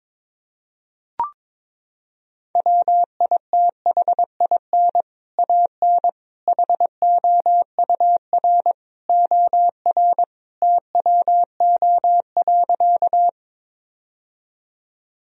Morse code, heard as a keyed tone, about 22 wpm, 700 Hz. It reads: WITHIN AN HOUR OR TWO.
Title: Sentences from Top 1000 Words - Review Part 4 - 22wpm